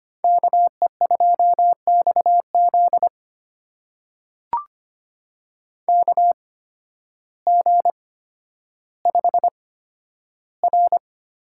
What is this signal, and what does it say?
Morse code, heard as a keyed tone, about 25 wpm, spaced as for 12 wpm, 700 Hz. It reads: KE2XZ KG5R
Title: USA Call Signs - 5 Characters - 8x Character Spacing - 25wpm